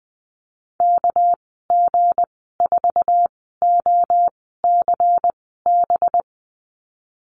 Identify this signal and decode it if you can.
Morse code, heard as a keyed tone, about 20 wpm, spaced as for 15 wpm, 700 Hz. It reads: KG4OCB
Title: 207 Lesson 33g - Letter j - Call Signs 20wpm